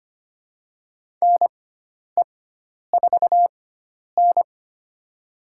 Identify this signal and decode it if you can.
Morse code, heard as a keyed tone, about 25 wpm, spaced as for 10 wpm, 700 Hz. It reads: NE4N
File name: Lesson 16g - Letter c - Call Signs 25wpm